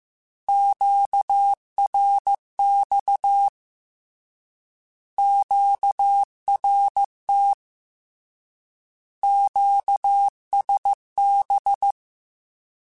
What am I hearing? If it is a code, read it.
Morse code, heard as a keyed tone, about 15 wpm, 800 Hz. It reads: QRX QRT QSB
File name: Short Runs / Short Runs-mp3s / Words @13